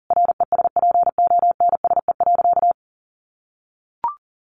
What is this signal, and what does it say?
Morse code, heard as a keyed tone, about 40 wpm, 700 Hz. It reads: RESPONSE.
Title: Sentences from Top 1400 Words 40wpm